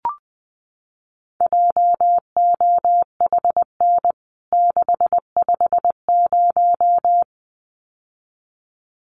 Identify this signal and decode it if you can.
Morse code, heard as a keyed tone, about 20 wpm, 700 Hz. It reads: JOHN 650
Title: CWT Exchange 20wpm